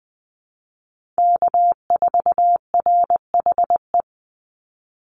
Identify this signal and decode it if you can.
Morse code, heard as a keyed tone, about 20 wpm, 700 Hz. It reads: K4RHE
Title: USA Call Signs 20wpm